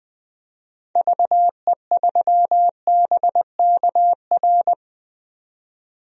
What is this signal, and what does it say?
Morse code, heard as a keyed tone, about 20 wpm, 700 Hz. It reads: VE3BKR
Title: DXCC-100 Call Signs - 6 Characters - 20wpm